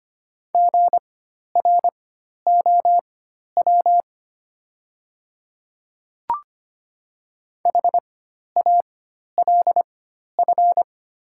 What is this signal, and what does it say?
Morse code, heard as a keyed tone, about 25 wpm, spaced as for 12 wpm, 700 Hz. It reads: GROW HALF
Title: 4x Character Spacing - Top 500 Words - 25wpm